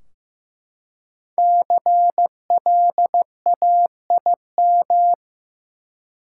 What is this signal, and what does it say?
Morse code, heard as a keyed tone, about 15 wpm, 700 Hz. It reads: CLAIM
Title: Five Letter Words - 15wpm